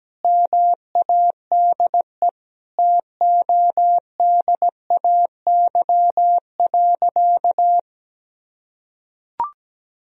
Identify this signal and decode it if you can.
Morse code, heard as a keyed tone, about 17 wpm, 700 Hz. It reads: MADE TODAY.
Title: Sentences from Top 1000 Words - Review Part 2 - 17wpm